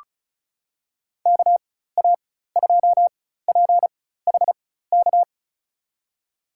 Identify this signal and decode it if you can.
Morse code, heard as a keyed tone, about 36 wpm, spaced as for 16 wpm, 700 Hz. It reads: KA2PHK
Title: USA Call Signs - 4x Character Spacing - 35wpm